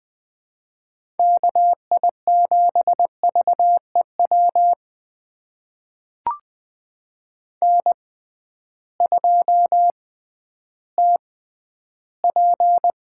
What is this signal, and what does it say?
Morse code, heard as a keyed tone, about 20 wpm, spaced as for 12 wpm, 700 Hz. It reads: KI7VEW N2TP
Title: USA Call Signs - 6x Character Spacing - 20wpm